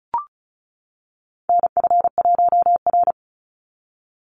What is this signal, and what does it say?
Morse code, heard as a keyed tone, about 34 wpm, 700 Hz. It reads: NF1R
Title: NA SOTA Chaser Callsigns 35wpm